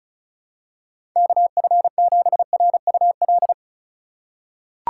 Morse code, heard as a keyed tone, about 34 wpm, 700 Hz. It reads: KF7RUL
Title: USA Call Signs - 6 Characters - 3x Character Spacing - 35wpm